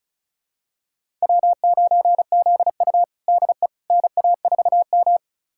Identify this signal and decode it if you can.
Morse code, heard as a keyed tone, about 36 wpm, 700 Hz. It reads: W9ZU DE NA4M
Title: USA Callsign DE Callsign - 4 Characters - 35wpm